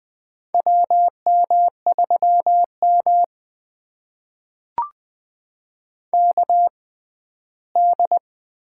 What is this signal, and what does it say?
Morse code, heard as a keyed tone, about 20 wpm, spaced as for 14 wpm, 700 Hz. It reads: WM3M KD
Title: USA Call Signs - 6x Character Spacing - 20wpm